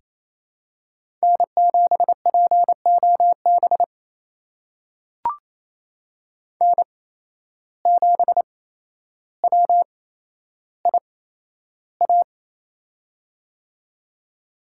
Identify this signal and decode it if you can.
Morse code, heard as a keyed tone, about 28 wpm, spaced as for 12 wpm, 700 Hz. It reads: N7POB N7WIA
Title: USA Call Signs - 5 Characters - 8x Character Spacing - 28wpm